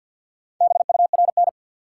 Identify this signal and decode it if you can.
Morse code, heard as a keyed tone, about 53 wpm, 700 Hz. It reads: BURN
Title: Sentences from Top 1300 Words - New Word Drill Alt - 50wpm